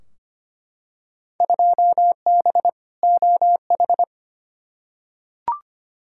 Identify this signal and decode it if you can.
Morse code, heard as a keyed tone, about 25 wpm, 700 Hz. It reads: 2B OH
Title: Field Day Exchanges Encore - 25wpm